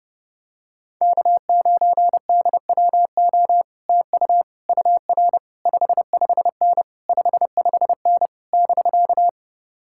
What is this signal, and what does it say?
Morse code, heard as a keyed tone, about 30 wpm, 700 Hz. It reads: K9DWO TU UR 55N 55N <BK>